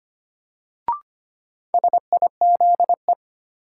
Morse code, heard as a keyed tone, about 25 wpm, 700 Hz. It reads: SIZE
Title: Sentences from Top 400 Words - New Word Drill - 25wpm